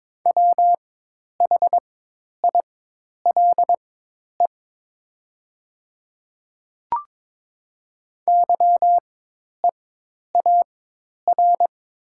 Morse code, heard as a keyed tone, about 22 wpm, spaced as for 10 wpm, 700 Hz. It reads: WHILE YEAR
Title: 4x Character Spacing - Top 500 Words - 22wpm